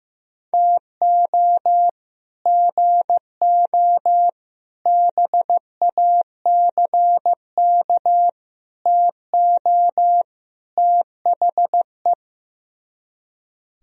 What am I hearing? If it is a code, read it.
Morse code, heard as a keyed tone, about 15 wpm, 700 Hz. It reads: TO GO BACK TO THE
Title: Speed-Racing - Sets of 5 Words 15wpm